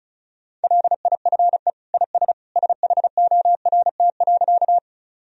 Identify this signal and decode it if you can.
Morse code, heard as a keyed tone, about 36 wpm, 700 Hz. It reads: LIFE IS SHORT.
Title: Sentences from Top 1000 Words - Review Part 2 - 35wpm